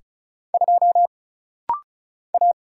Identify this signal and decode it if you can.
Morse code, heard as a keyed tone, about 34 wpm, 700 Hz. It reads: 2 A
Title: Single Letter-Number - Rapid-Fire 35wpm